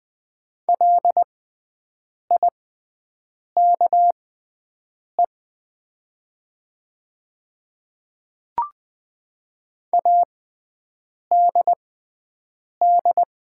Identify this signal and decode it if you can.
Morse code, heard as a keyed tone, about 20 wpm, spaced as for 7 wpm, 700 Hz. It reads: LIKE ADD
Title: 6x Character Spacing - Top 200 Words - 20wpm